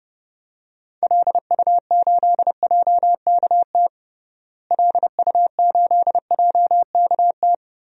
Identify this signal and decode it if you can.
Morse code, heard as a keyed tone, about 32 wpm, 700 Hz. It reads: LU8JKT LU8JKT
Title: DXCC-100 Call Signs - 6 Characters - Repeated 3x - 30wpm